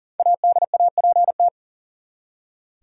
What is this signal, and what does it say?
Morse code, heard as a keyed tone, about 40 wpm, 700 Hz. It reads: ADAPT
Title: Five Letter Words - 40wpm